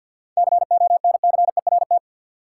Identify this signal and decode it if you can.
Morse code, heard as a keyed tone, about 50 wpm, 700 Hz. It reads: CONCERT
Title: Seven Letter Words - 50wpm